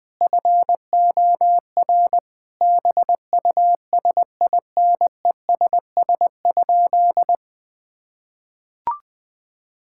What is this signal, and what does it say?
Morse code, heard as a keyed tone, about 21 wpm, 700 Hz. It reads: FOR BUSINESS?
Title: Sentences from Top 1000 Words - Review Part 6 - 20wpm